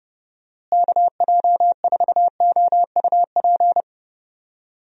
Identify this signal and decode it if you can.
Morse code, heard as a keyed tone, about 31 wpm, 700 Hz. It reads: KJ4OUP
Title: USA Call Signs - 6 Characters - Repeated 3x - 30wpm